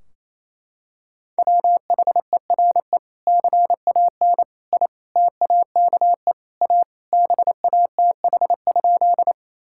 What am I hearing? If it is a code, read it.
Morse code, heard as a keyed tone, about 28 wpm, 700 Hz. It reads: WHERE CAN I TAKE A BATH?
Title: Sentences from Top 1600 Words 28wpm